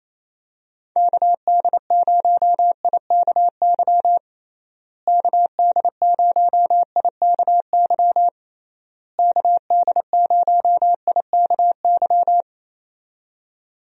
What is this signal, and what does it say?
Morse code, heard as a keyed tone, about 28 wpm, 700 Hz. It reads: KD0IKY KD0IKY KD0IKY
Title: USA Call Signs - 6 Characters - Repeated 3x - 28wpm